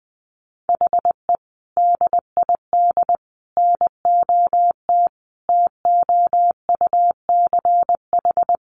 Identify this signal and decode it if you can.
Morse code, heard as a keyed tone, about 19 wpm, 700 Hz. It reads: HE DID NOT TOUCH
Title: Sentences from Top 1500 Words 20wpm